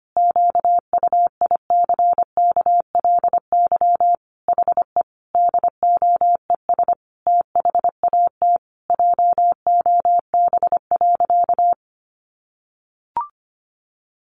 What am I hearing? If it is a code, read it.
Morse code, heard as a keyed tone, about 25 wpm, 700 Hz. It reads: QUICKLY HE DOES THAT JOB.
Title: Sentences from Top 700 Words 25wpm